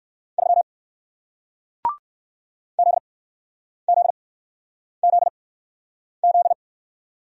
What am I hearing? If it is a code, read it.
Morse code, heard as a keyed tone, about 56 wpm, spaced as for 9 wpm, 700 Hz. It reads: 3 ZZZZ
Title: Speed-Racing Single Letter-Number 45wpm